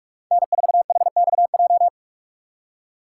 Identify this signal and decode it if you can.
Morse code, heard as a keyed tone, about 47 wpm, 700 Hz. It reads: N4HXJ
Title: USA Call Signs - 2x Character Spacing - 45wpm